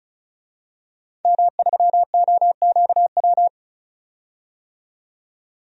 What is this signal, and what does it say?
Morse code, heard as a keyed tone, about 36 wpm, 700 Hz. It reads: M3OQW
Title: DXCC-100 Call Signs - 5 Characters - 35wpm